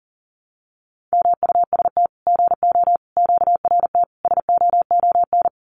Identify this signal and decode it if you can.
Morse code, heard as a keyed tone, about 40 wpm, 700 Hz. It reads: MUST GO QRT SOON
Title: Common QSO Phrases 40wpm